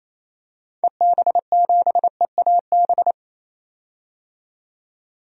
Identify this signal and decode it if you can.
Morse code, heard as a keyed tone, about 27 wpm, 700 Hz. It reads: EB7EAB
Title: DXCC-100 Call Signs - 6 Characters - 28wpm